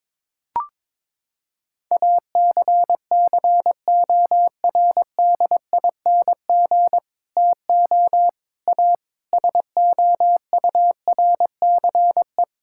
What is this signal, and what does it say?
Morse code, heard as a keyed tone, about 22 wpm, 700 Hz. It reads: ACCORDING TO A SOURCE
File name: Sentences with 9-Word Phrases - 22wpm